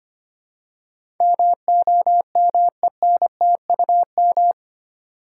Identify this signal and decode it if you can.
Morse code, heard as a keyed tone, about 25 wpm, 700 Hz. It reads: MOMENTUM
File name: 8 Letter Words - First 3 Letters Given - 25wpm